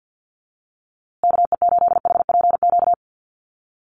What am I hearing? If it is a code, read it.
Morse code, heard as a keyed tone, about 49 wpm, 700 Hz. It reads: KE8HPQ
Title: USA Call Signs Repeated 3x 50wpm